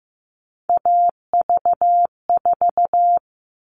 Morse code, heard as a keyed tone, about 15 wpm, 700 Hz. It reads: AV4
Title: All Cut Numbers - 3-Digit 15wpm